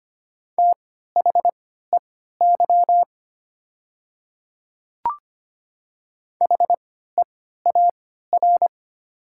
Morse code, heard as a keyed tone, about 25 wpm, spaced as for 14 wpm, 700 Hz. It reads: THEY HEAR
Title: 3x Character Spacing - Top 300 Words - 25wpm